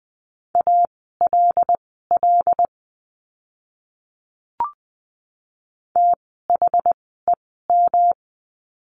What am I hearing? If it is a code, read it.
Morse code, heard as a keyed tone, about 20 wpm, spaced as for 15 wpm, 700 Hz. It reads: ALL THEM
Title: 146 Lesson 24f - Letter y - Words 20wpm